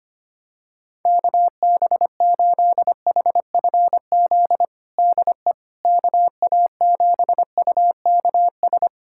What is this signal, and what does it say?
Morse code, heard as a keyed tone, about 25 wpm, 700 Hz. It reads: KB8HFZ DE KA7UKS
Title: USA Callsign DE Callsign - 6 Characters - 25wpm